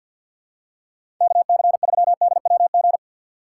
Morse code, heard as a keyed tone, about 51 wpm, 700 Hz. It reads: KC3DWG
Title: USA Call Signs - 4x Character Spacing - 50wpm